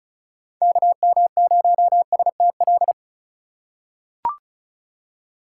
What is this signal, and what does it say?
Morse code, heard as a keyed tone, about 34 wpm, 700 Hz. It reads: KM0STL
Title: USA Call Signs - 6 Characters - 4x Character Spacing - 35wpm